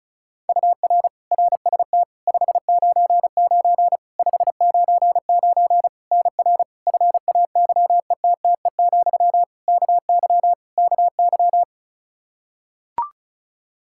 Morse code, heard as a keyed tone, about 36 wpm, 700 Hz. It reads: UR RST 599 599 NR FAYETTE, KY KY
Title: Practice QSO - Exchange 1 - Beginning - 35wpm